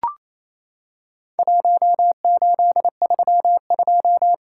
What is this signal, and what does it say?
Morse code, heard as a keyed tone, about 28 wpm, 700 Hz. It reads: 1832
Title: Worked All Europe DX Contest - QTC Exchange - 28wpm